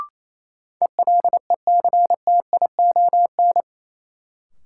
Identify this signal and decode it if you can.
Morse code, heard as a keyed tone, about 28 wpm, 700 Hz. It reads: ELECTION
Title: Sentences from Top 1500 Words - New Word Drill Alt - 28wpm